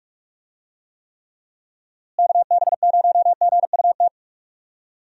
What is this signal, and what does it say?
Morse code, heard as a keyed tone, about 46 wpm, 700 Hz. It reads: KB0GUT
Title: USA Call Signs - 6 Characters - Repeated 3x - 45wpm